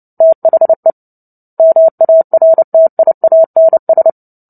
Morse code, heard as a keyed tone, about 30 wpm, 650 Hz. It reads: THE MARTIANS
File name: Book Two / 03-Chapter 3 - 29 WPM